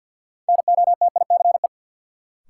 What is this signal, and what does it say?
Morse code, heard as a keyed tone, about 52 wpm, 700 Hz. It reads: NOTICE
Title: Sentences from Top 400 Words - New Word Drill - 50wpm